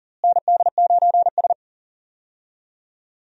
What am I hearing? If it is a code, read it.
Morse code, heard as a keyed tone, about 40 wpm, 700 Hz. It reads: ND9S
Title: USA Call Signs - 4 Characters - Repeated 3x - 40wpm